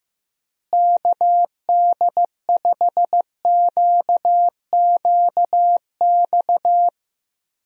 Morse code, heard as a keyed tone, about 15 wpm, 700 Hz. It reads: KD5QQX
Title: USA Call Signs - 6 Characters - Repeated 2x - 15wpm